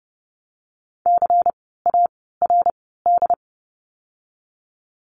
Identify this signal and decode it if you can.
Morse code, heard as a keyed tone, about 30 wpm, 700 Hz. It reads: C A R D
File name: Lesson 16f - Letter c - Words 30wpm